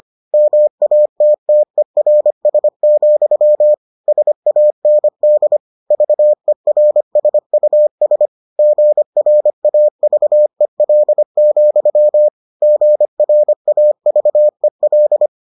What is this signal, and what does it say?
Morse code, heard as a keyed tone, about 25 wpm, 600 Hz. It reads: MATTERS, SAND VERSUS GRAVEL, GRAVEL